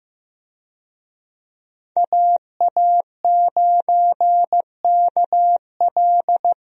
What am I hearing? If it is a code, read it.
Morse code, heard as a keyed tone, about 15 wpm, 700 Hz. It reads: AA9KL
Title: USA Call Signs - 5 Characters - 15wpm